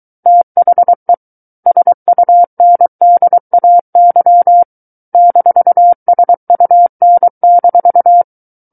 Morse code, heard as a keyed tone, about 23 wpm, 700 Hz. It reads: THE SUNDAY -SUN-
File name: Book One / 14-Chapter 14 - 23 WPM